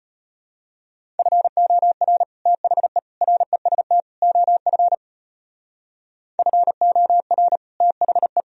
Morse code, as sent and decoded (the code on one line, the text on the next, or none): ..-. --- .-. / - .... / .-. ... - / --- ..-. / ..-. --- .-. / - .... .
FOR THE REST OF FOR THE